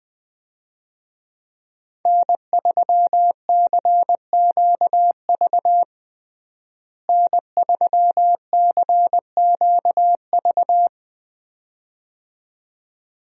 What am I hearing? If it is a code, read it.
Morse code, heard as a keyed tone, about 20 wpm, 700 Hz. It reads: N3CQV N3CQV
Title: USA Call Signs - 5 Characters - Repeated 2x - 20wpm